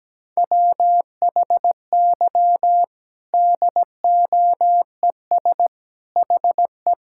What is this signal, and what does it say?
Morse code, heard as a keyed tone, about 17 wpm, 700 Hz. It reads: WHY DOES HE
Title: Sentences from Top 1600 Words 17wpm